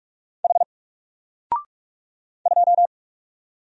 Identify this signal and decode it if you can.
Morse code, heard as a keyed tone, about 44 wpm, 700 Hz. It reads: H 2
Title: Single Character (Letter:Number:Punctuation:Prosign) Alt 45wpm